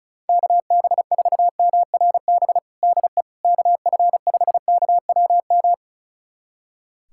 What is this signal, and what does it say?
Morse code, heard as a keyed tone, about 36 wpm, 700 Hz. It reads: KB4MRB DE KF5KWM
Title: USA Callsign DE Callsign 35wpm